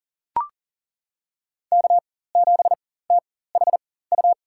Morse code, heard as a keyed tone, about 40 wpm, spaced as for 19 wpm, 700 Hz. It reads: K7THU
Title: USA Call Signs - 5 Characters - 4x Character Spacing - 40wpm